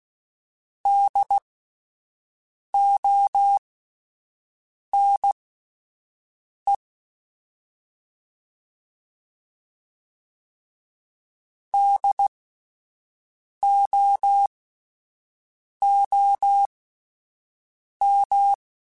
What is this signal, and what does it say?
Morse code, heard as a keyed tone, about 16 wpm, spaced as for 5 wpm, 800 Hz. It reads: DONE DOOM